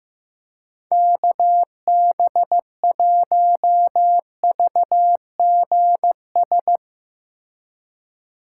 Morse code, heard as a keyed tone, about 16 wpm, 700 Hz. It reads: KB1VGS
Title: Speed Racing - USA 2x3 Call Signs 15wpm